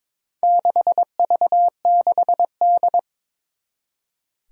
Morse code, heard as a keyed tone, about 22 wpm, 700 Hz. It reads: 6V6D
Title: Four Challenging Letter-Number Combinations 22wpm